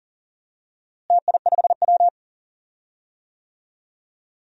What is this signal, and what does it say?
Morse code, heard as a keyed tone, about 39 wpm, 700 Hz. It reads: TI5W
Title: DXCC-100 Call Signs - 4 Characters - Repeated 2x - 40wpm